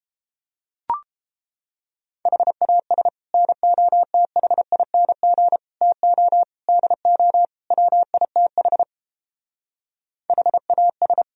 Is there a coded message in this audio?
Yes: Morse code from 2.3 to 11.2 s, heard as a keyed tone, about 29 wpm, 700 Hz. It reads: HAS NOTHING TO DO WITH HAS